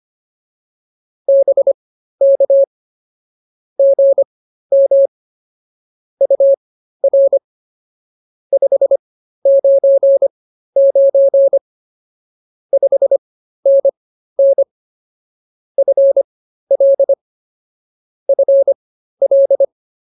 Morse code, heard as a keyed tone, about 24 wpm, spaced as for 13 wpm, 550 Hz. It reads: BK GM UR 599 5NN FL FL